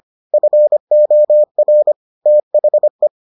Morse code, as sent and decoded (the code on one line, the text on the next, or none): ..-. --- .-. / - .... .
FOR THE